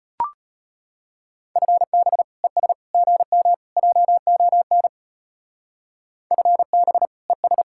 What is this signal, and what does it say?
Morse code, heard as a keyed tone, about 29 wpm, 700 Hz. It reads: FB ES GM JON FB ES
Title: Speed-Racing Common QSO Phrases - 25wpm